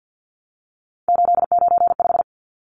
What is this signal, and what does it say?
Morse code, heard as a keyed tone, about 48 wpm, 700 Hz. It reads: 895
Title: Numbers - 3-Digit 50wpm